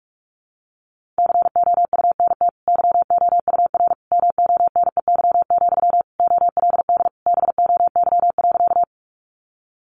Morse code, heard as a keyed tone, about 44 wpm, 700 Hz. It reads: COUNT YOUR MONEY, OLD BOY.
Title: Sentences from Top 1500 Words 45wpm